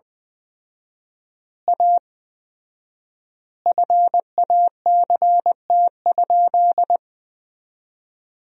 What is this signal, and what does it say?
Morse code, heard as a keyed tone, about 20 wpm, 700 Hz. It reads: A FACT?
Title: 4x Word-Spacing - Sentences from Top 500 Words - 20wpm